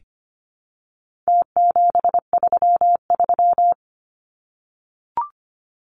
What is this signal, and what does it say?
Morse code, heard as a keyed tone, about 25 wpm, 700 Hz. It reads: T733
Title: Numbers - Contest Style - 4-Digit 25wpm